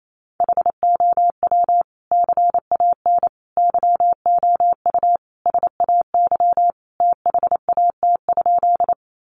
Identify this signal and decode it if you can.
Morse code, heard as a keyed tone, about 28 wpm, 700 Hz. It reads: HOW CAN YOU SAY THAT?
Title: Sentences from Top 200 Words 28wpm